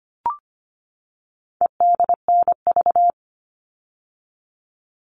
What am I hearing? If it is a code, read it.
Morse code, heard as a keyed tone, about 25 wpm, 700 Hz. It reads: EDNV
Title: All Cut Numbers - 4-Digit 25wpm